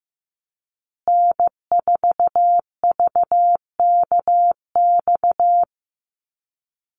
Morse code, heard as a keyed tone, about 15 wpm, 700 Hz. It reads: N4VKX
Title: USA Call Signs Repeated 3x 15wpm